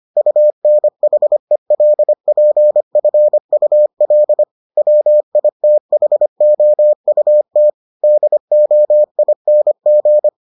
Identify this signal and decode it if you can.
Morse code, heard as a keyed tone, about 25 wpm, 600 Hz. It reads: UNHELPFUL WITHOUT DOING